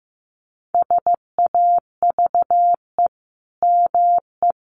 Morse code, heard as a keyed tone, about 15 wpm, 700 Hz. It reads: SAVE ME
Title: Sentences from Top 1100 Words 15wpm